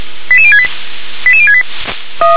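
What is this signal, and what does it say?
Five-tone selective call (ZVEI), heard as quick runs of five tones, tones of about 70 ms. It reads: 88068 88068
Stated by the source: ZVEI five-tone selective call received by SDR.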